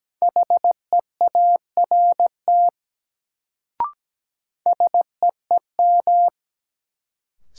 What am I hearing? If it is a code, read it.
Morse code, heard as a keyed tone, about 17 wpm, 700 Hz. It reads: HEART SEEM
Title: Sets of 3 Words - New Word Drill - 17wpm